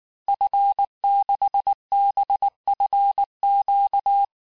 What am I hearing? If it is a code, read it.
Morse code, heard as a keyed tone, about 19 wpm, 800 Hz. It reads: F6BFQ
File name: cours-du-lundi files / cours-1589-texte-FIN-de-QSO-vitesse-18mpm